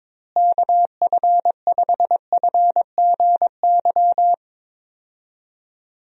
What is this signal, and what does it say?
Morse code, heard as a keyed tone, about 22 wpm, 700 Hz. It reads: KF5FGY
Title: USA Call Signs - 6 Characters - Repeated 2x - 22wpm